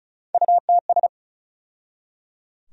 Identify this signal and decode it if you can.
Morse code, heard as a keyed tone, about 36 wpm, 700 Hz. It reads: UTS
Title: USA Call Sign Postfixes - 3 Letters 35wpm